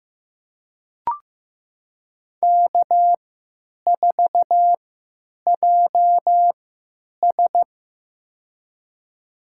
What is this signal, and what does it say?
Morse code, heard as a keyed tone, about 16 wpm, spaced as for 9 wpm, 700 Hz. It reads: K4JS
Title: USA Call Signs - 4 Characters - 3x Character Spacing - 15wpm